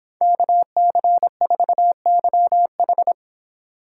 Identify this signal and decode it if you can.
Morse code, heard as a keyed tone, about 25 wpm, 700 Hz. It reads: KC4YH